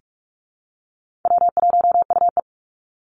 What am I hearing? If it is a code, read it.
Morse code, heard as a keyed tone, about 44 wpm, 700 Hz. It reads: W1UE
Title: Easy Call Signs Repeated 3x - 45wpm